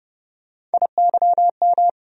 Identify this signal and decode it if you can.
Morse code, heard as a keyed tone, about 30 wpm, 700 Hz. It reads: IYM
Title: USA Call Sign Postfixes - 3 Letters 30wpm